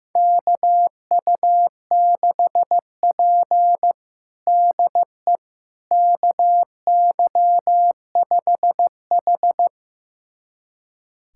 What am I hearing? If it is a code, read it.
Morse code, heard as a keyed tone, about 15 wpm, 700 Hz. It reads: KU6P DE KY5H
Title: USA Callsign DE Callsign - 4 Characters - 15wpm